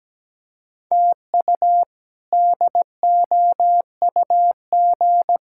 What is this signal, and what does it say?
Morse code, heard as a keyed tone, about 17 wpm, 700 Hz. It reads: TU DOUG
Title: K1USN SST QSO Practice - 17wpm